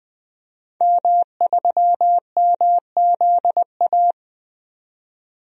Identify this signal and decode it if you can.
Morse code, heard as a keyed tone, about 20 wpm, 700 Hz. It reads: M3MZA
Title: DXCC-100 Call Signs - 5 Characters - 20wpm